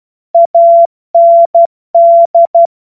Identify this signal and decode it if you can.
Morse code, heard as a keyed tone, about 12 wpm, 680 Hz. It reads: AND
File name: Morsekurs / Witze